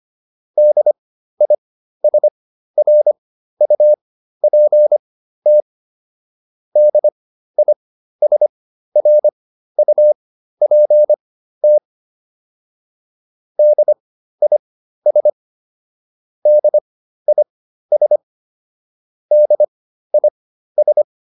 Morse code, heard as a keyed tone, about 25 wpm, spaced as for 13 wpm, 600 Hz. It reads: DISRUPT DISRUPT DIS DIS DIS
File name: DIS7 13